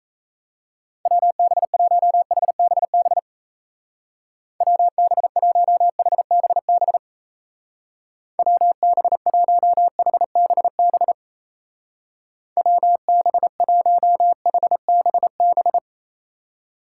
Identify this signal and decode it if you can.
Morse code, heard as a keyed tone, about 36 wpm, 700 Hz. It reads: WB1HBB WB1HBB WB1HBB WB1HBB